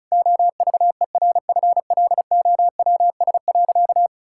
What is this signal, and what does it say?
Morse code, heard as a keyed tone, about 36 wpm, 700 Hz. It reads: OVERFLOWS.
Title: Pangram Challenge v2 - 35wpm